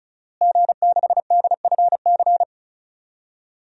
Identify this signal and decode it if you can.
Morse code, heard as a keyed tone, about 36 wpm, 700 Hz. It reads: G6DFC
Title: DXCC-100 Call Signs - 5 Characters - Repeated 3x - 35wpm